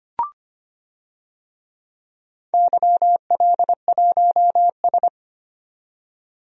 Morse code, heard as a keyed tone, about 25 wpm, 700 Hz. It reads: YL1S
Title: DXCC-100 Call Signs - 4 Characters - 25wpm